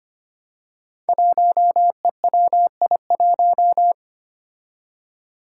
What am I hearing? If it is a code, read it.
Morse code, heard as a keyed tone, about 25 wpm, 700 Hz. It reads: 1EWI1
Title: Five Challenging Letter-Number Combinations 25wpm